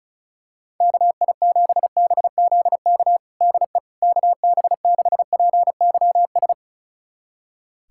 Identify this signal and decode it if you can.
Morse code, heard as a keyed tone, about 36 wpm, 700 Hz. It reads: KI7BZK DE KB6PYS
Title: USA Callsign DE Callsign - 6 Characters - 35wpm